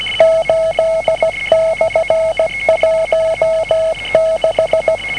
Morse code, heard as a keyed tone, about 17 wpm, 650 Hz, with noise. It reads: 8/16